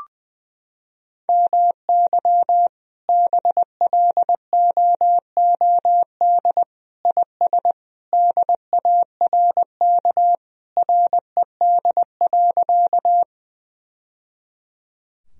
Morse code, heard as a keyed tone, about 19 wpm, 700 Hz. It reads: MY BLOOD IS DARK RED.
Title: Sentences from Top 1000 Words - Review Part 2 - 20wpm